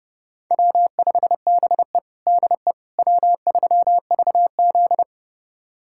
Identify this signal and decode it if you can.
Morse code, heard as a keyed tone, about 32 wpm, 700 Hz. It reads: W5BE DE W3VZ
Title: USA Callsign DE Callsign - 4 Characters - 30wpm